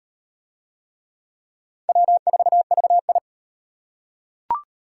Morse code, heard as a keyed tone, about 33 wpm, 700 Hz. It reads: W4VI